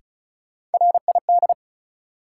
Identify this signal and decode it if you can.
Morse code, heard as a keyed tone, about 36 wpm, 700 Hz. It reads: RID
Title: Sets of 4 Words - New Word Drill - 35wpm